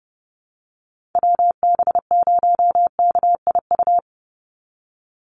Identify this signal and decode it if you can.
Morse code, heard as a keyed tone, about 30 wpm, 700 Hz. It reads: WB0KIU
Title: NA SOTA Chaser Callsigns 30wpm